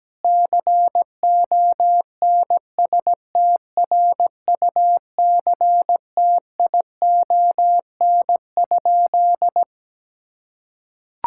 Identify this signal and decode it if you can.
Morse code, heard as a keyed tone, about 17 wpm, 700 Hz. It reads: CONSTRUCTION?
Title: Sentences from Top 1600 Words 17wpm